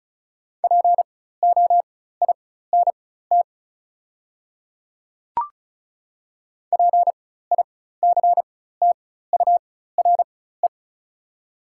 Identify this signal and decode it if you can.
Morse code, heard as a keyed tone, about 34 wpm, spaced as for 16 wpm, 700 Hz. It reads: POINT PICTURE
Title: 4x Character Spacing - Top 300 Words - 35wpm